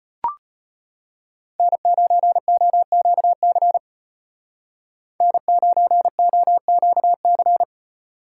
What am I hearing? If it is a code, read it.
Morse code, heard as a keyed tone, about 31 wpm, 700 Hz. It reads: N9OQC N9OQC
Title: Speed Racing - USA Call Signs 25wpm